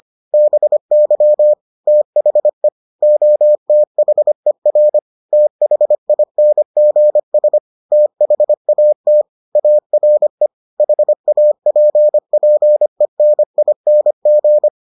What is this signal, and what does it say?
Morse code, heard as a keyed tone, about 25 wpm, 600 Hz. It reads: BY THE OTHER THINGS THAT ARE HAPPENING